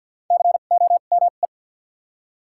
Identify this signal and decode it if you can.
Morse code, heard as a keyed tone, about 48 wpm, spaced as for 35 wpm, 700 Hz. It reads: COME
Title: 2x Character Spacing - Top 100 Words - 50wpm